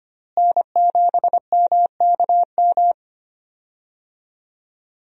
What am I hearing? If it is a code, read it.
Morse code, heard as a keyed tone, about 25 wpm, 700 Hz. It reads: N7MKM
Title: USA Call Signs - 5 Characters - Repeated 2x - 25wpm